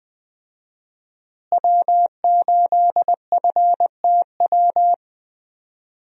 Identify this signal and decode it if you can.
Morse code, heard as a keyed tone, about 21 wpm, 700 Hz. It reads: W8FTW
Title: USA Call Signs - 5 Characters - Repeated 2x - 20wpm